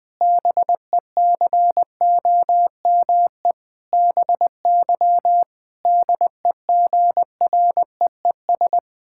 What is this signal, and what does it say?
Morse code, heard as a keyed tone, about 20 wpm, 700 Hz. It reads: BECOME BY DEGREES
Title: Pangram Challenge v2 - 20wpm